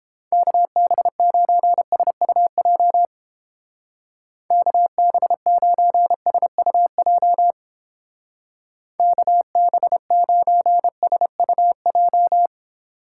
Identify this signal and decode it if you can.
Morse code, heard as a keyed tone, about 29 wpm, 700 Hz. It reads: KB9SUJ KB9SUJ KB9SUJ